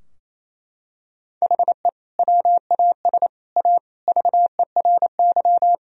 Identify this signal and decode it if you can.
Morse code, heard as a keyed tone, about 27 wpm, 700 Hz. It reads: HE WAS A VERY
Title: Sentences from Top 1000 Words - Review Part 3 - 28wpm